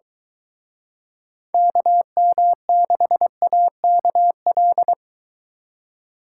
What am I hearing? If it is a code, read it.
Morse code, heard as a keyed tone, about 22 wpm, 700 Hz. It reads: KM6AKL